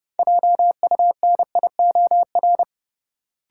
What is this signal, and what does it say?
Morse code, heard as a keyed tone, about 29 wpm, 700 Hz. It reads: JUNIOR
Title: Sentences from Top 1000 Words - Review Part 7 - New Word Drill - 30wpm